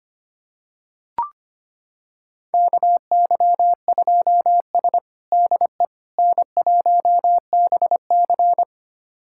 Morse code, heard as a keyed tone, about 25 wpm, 700 Hz. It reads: KY2S DE N1BC
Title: USA Callsign DE Callsign - 4 Characters - 25wpm